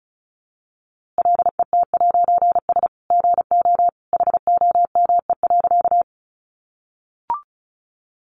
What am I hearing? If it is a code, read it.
Morse code, heard as a keyed tone, about 34 wpm, 700 Hz. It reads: LET'S GO HOME.
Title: A Sound Of Thunder 35wpm